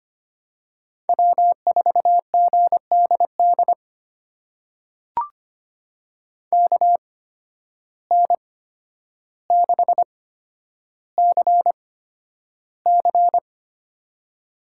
Morse code, heard as a keyed tone, about 25 wpm, spaced as for 11 wpm, 700 Hz. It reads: W4GDD KN6CC
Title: USA Call Signs - 5 Characters - 8x Character Spacing - 25wpm